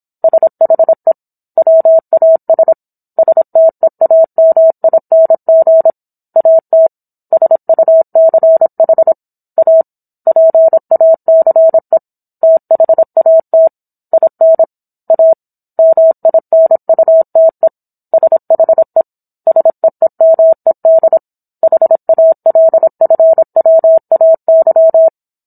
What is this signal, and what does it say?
Morse code, heard as a keyed tone, about 27 wpm, 650 Hz. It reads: SHE WAS STEAMING AT SUCH A PACE THAT IN A MINUTE SHE SEEMED HALFWAY